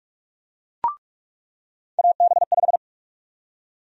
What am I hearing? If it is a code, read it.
Morse code, heard as a keyed tone, about 46 wpm, 700 Hz. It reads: AB5
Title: USA Call Sign Prefixes - 45wpm